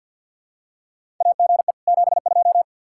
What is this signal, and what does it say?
Morse code, heard as a keyed tone, about 49 wpm, 700 Hz. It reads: AGE 72
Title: K1USN SST QSO Practice 50wpm